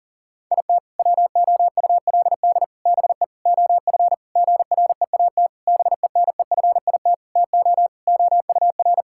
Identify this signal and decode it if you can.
Morse code, heard as a keyed tone, about 40 wpm, 700 Hz. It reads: IT WOULD BE OF GREAT BENEFIT TO OUR
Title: Sentences from Top 1000 Words - Review Part 6 - 40wpm